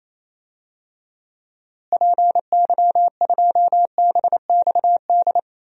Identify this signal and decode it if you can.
Morse code, heard as a keyed tone, about 28 wpm, 700 Hz. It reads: PY2BXD
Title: DXCC-100 Call Signs - 6 Characters - Repeated 2x - 28wpm